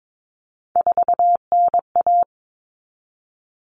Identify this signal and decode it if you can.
Morse code, heard as a keyed tone, about 22 wpm, 700 Hz. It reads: 4NA
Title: Common Cut Numbers - 3-Digit 22wpm